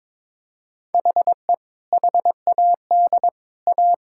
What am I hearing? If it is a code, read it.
Morse code, heard as a keyed tone, about 22 wpm, 700 Hz. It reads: HE HAD A
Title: Sentences from Top 1000 Words - Review Part 6 - 22wpm